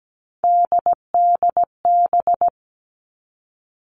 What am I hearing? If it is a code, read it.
Morse code, heard as a keyed tone, about 17 wpm, 700 Hz. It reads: DDB
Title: All Cut Numbers - 3-Digit 17wpm